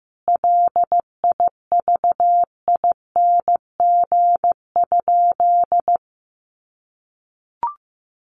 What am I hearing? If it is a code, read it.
Morse code, heard as a keyed tone, about 15 wpm, 700 Hz. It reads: LIVING?
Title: Sentences from Top 1100 Words 15wpm